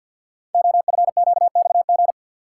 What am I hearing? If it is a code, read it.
Morse code, heard as a keyed tone, about 52 wpm, 700 Hz. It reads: OFQXG
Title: Code Groups - 5 Letters - 50wpm